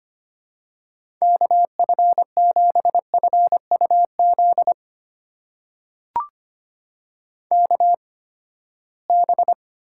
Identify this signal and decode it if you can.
Morse code, heard as a keyed tone, about 25 wpm, spaced as for 17 wpm, 700 Hz. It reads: KF7FUZ KB
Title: USA Call Signs - 6 Characters - 8x Character Spacing - 25wpm